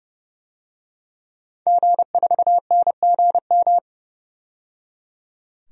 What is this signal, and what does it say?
Morse code, heard as a keyed tone, about 29 wpm, 700 Hz. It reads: G4NGM
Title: DXCC-100 Call Signs - 5 Characters - 30wpm